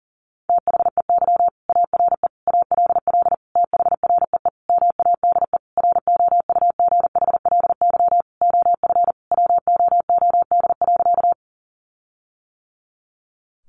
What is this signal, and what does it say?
Morse code, heard as a keyed tone, about 40 wpm, 700 Hz. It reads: THEY ARE ALL THREE MADE ROUGHLY OF WOOD.
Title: Sentences from Top 1500 Words 40wpm